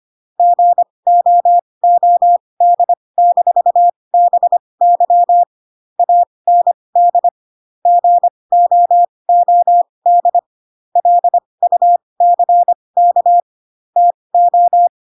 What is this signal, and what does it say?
Morse code, heard as a keyed tone, about 25 wpm, spaced as for 20 wpm, 700 Hz. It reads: GOOD-BY AND GOOD LUCK TO